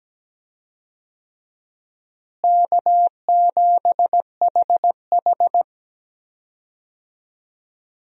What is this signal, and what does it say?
Morse code, heard as a keyed tone, about 17 wpm, 700 Hz. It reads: K7HH
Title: USA Call Signs - 4 Characters - Repeated 2x - 17wpm